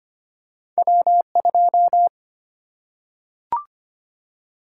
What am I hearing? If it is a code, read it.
Morse code, heard as a keyed tone, about 25 wpm, 700 Hz. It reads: W2
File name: Two Letter-Number Combinations - 25wpm